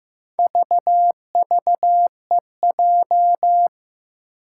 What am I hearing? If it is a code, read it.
Morse code, heard as a keyed tone, about 15 wpm, 700 Hz. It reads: VVEJ
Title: Four Challenging Letter-Number Combinations 15wpm